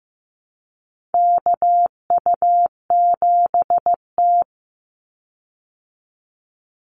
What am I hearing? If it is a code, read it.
Morse code, heard as a keyed tone, about 15 wpm, 700 Hz. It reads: KU7T
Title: Call Signs Repeated 3x - 15wpm